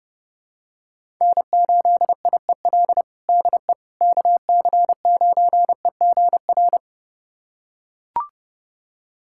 Morse code, heard as a keyed tone, about 30 wpm, 700 Hz. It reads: N8IEL DE KC9EGR
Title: USA Callsign DE Callsign 30wpm